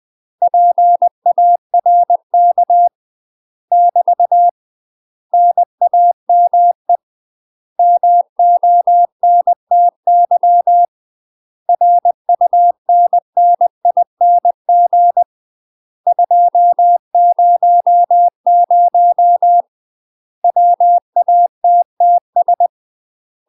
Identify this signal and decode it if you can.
Morse code, heard as a keyed tone, about 20 wpm, 700 Hz. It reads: PARK <BT> NAME MONTY RUNNING 200 WATTS
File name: UK MASTER QSOs TRAINING / 20WPM / UK MASTER QSOs 0037